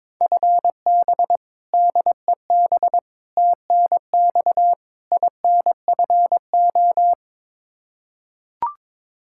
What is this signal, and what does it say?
Morse code, heard as a keyed tone, about 22 wpm, 700 Hz. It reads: FB DEB TNX INFO
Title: Practice QSO - Exchange 3 - More Info - 22wpm